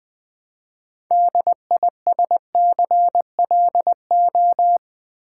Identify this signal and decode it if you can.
Morse code, heard as a keyed tone, about 20 wpm, 700 Hz. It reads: DISCLO
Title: Build A Word - 3 Letters At A Time - 8-Letter Words - 20wpm